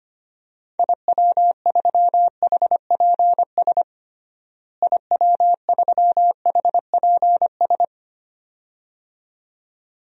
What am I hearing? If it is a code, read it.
Morse code, heard as a keyed tone, about 25 wpm, 700 Hz. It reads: IW3HPS IW3HPS
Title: DXCC-100 Call Signs - 6 Characters - Repeated 2x - 25wpm